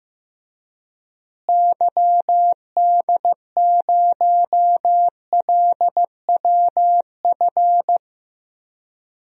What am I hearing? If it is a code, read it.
Morse code, heard as a keyed tone, about 15 wpm, 700 Hz. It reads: YD0LWF
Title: DXCC-100 Call Signs - 6 Characters - Repeated 3x - 15wpm